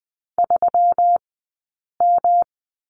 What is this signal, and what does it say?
Morse code, heard as a keyed tone, about 20 wpm, spaced as for 8 wpm, 700 Hz. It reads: 3M
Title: 165 Lesson 27d - Number 7 - Two Characters 20wpm